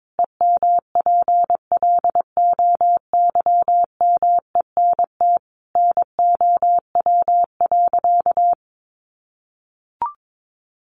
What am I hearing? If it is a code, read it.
Morse code, heard as a keyed tone, about 22 wpm, 700 Hz. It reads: EMPLOYMENT NOW.
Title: Sentences from Top 1500 Words 22wpm